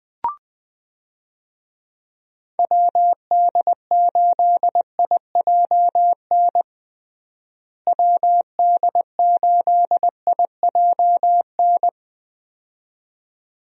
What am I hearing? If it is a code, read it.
Morse code, heard as a keyed tone, about 21 wpm, 700 Hz. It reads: WD8IJN WD8IJN
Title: USA Call Signs - 6 Characters - Repeated 2x - 20wpm